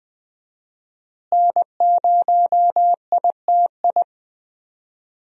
Morse code, heard as a keyed tone, about 20 wpm, 700 Hz. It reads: N0ITI
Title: USA Call Signs - 5 Characters - Repeated 2x - 20wpm